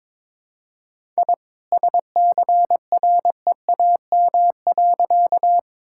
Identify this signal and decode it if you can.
Morse code, heard as a keyed tone, about 22 wpm, 700 Hz. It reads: I SCREAM.
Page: morse-code-ninja-practice-22wpm